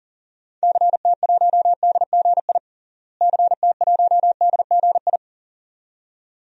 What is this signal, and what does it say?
Morse code, heard as a keyed tone, about 37 wpm, 700 Hz. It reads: CT1DGI CT1DGI
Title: DXCC-100 Call Signs - 6 Characters - Repeated 2x - 40wpm